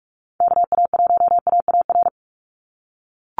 Morse code, heard as a keyed tone, about 44 wpm, 700 Hz. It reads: KA1AAR
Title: USA Call Signs 45wpm